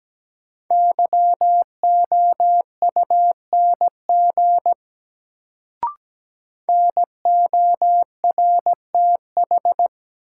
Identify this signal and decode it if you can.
Morse code, heard as a keyed tone, about 17 wpm, 700 Hz. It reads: YOUNG NORTH
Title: Sentences from Top 400 Words - New Word Drill - 17wpm